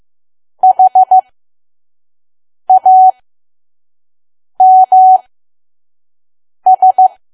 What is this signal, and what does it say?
Morse code, heard as a keyed tone, about 15 wpm, spaced as for 5 wpm, 750 Hz. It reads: HAMS
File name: Archive / 5 WPM files / PM